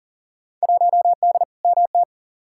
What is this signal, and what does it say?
Morse code, heard as a keyed tone, about 40 wpm, 700 Hz. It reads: 1D MT
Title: Field Day Exchanges - 40wpm